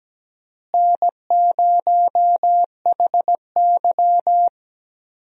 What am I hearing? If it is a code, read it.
Morse code, heard as a keyed tone, about 17 wpm, 700 Hz. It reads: N0HY
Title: USA Call Signs - 4 Characters - Repeated 3x - 17wpm